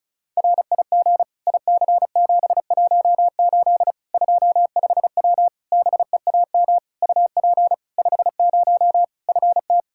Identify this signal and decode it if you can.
Morse code, heard as a keyed tone, about 34 wpm, 700 Hz. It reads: RIG IC718 25W BEAM UP 50 FT
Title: K1USN SST QSO Practice 35wpm